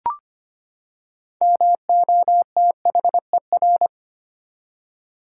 Morse code, heard as a keyed tone, about 25 wpm, 700 Hz. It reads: MOTHER
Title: Six Letter Words - 25wpm